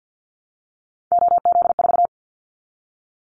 Morse code, heard as a keyed tone, about 50 wpm, 700 Hz. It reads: OZ4
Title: Callsign Prefixes for Top 50 DXCC - 50wpm